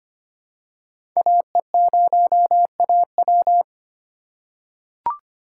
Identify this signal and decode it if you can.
Morse code, heard as a keyed tone, about 25 wpm, 700 Hz. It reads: AE0AW
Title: USA Call Signs - 5 Characters - 2x Character Spacing - 25wpm